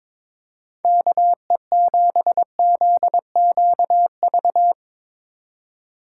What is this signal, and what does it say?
Morse code, heard as a keyed tone, about 22 wpm, 700 Hz. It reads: KE7ZQV
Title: USA Call Signs - 2x Character Spacing - 22wpm